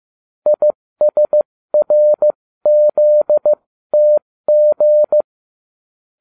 Morse code, heard as a keyed tone, about 15 wpm, spaced as for 13 wpm, 600 Hz. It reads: ISRZTG